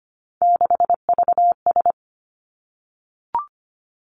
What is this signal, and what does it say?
Morse code, heard as a keyed tone, about 25 wpm, 700 Hz. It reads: 6VS
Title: Three Challenging Letter-Number Combinations 25wpm